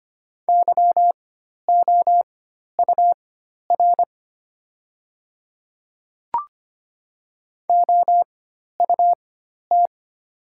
Morse code, heard as a keyed tone, about 25 wpm, spaced as for 12 wpm, 700 Hz. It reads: YOUR OUT
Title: 4x Character Spacing - Top 100 Words - 25wpm